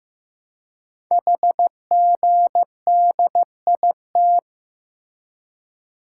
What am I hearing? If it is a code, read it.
Morse code, heard as a keyed tone, about 16 wpm, 700 Hz. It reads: HGDIT
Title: Code Groups - 5 Letters - 15wpm